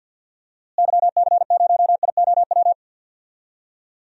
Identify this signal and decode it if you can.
Morse code, heard as a keyed tone, about 50 wpm, 700 Hz. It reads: YC0IOW
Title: DXCC-100 Call Signs - 6 Characters - Repeated 3x - 50wpm